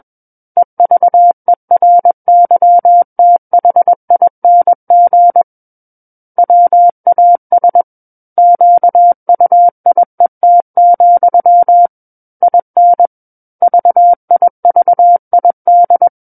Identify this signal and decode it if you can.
Morse code, heard as a keyed tone, about 21 wpm, 700 Hz. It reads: EVERYTHING WAS QUIET, IN VIVID
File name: Book One / 12-Chapter 12 - 21 WPM